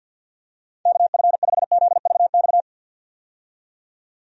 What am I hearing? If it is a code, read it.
Morse code, heard as a keyed tone, about 46 wpm, 700 Hz. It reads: KF5ZVX
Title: USA Call Signs - 6 Characters - 50wpm